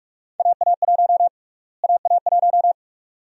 Morse code, heard as a keyed tone, about 46 wpm, 700 Hz. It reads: AA1 AA1
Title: USA Call Sign Prefixes - Repeated 2x - 45wpm